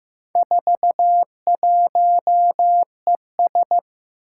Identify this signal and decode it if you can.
Morse code, heard as a keyed tone, about 15 wpm, 700 Hz. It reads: 41ES
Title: Four Challenging Letter-Number Combinations 15wpm